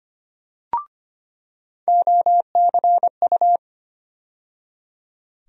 Morse code, heard as a keyed tone, about 25 wpm, 700 Hz. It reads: OCU
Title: USA Call Sign Postfixes - 3 Letters 25wpm